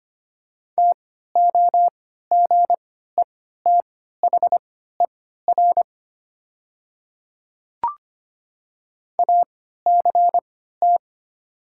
Morse code, heard as a keyed tone, about 25 wpm, spaced as for 14 wpm, 700 Hz. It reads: TOGETHER ACT
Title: 3x Character Spacing - Top 500 Words - 25wpm